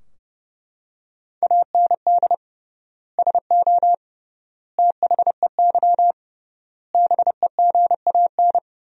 Morse code, heard as a keyed tone, about 30 wpm, 700 Hz. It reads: AND SO THEY BEGAN
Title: 3x Word-Spacing - Sentences from Top 400 Words - 30wpm